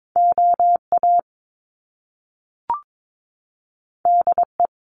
Morse code, heard as a keyed tone, about 22 wpm, 700 Hz. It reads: OA DE
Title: Top Two Letter Combinations in Words 22wpm